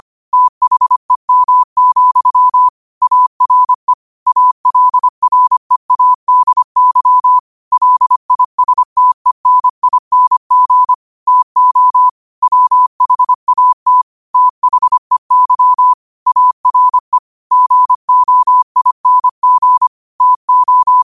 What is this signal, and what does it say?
Morse code, heard as a keyed tone, about 25 wpm, 1000 Hz. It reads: THEM, ARE ALREADY LISTENING TO WHAT THEY ARE GOING TO
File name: qotdfiles / 25wpm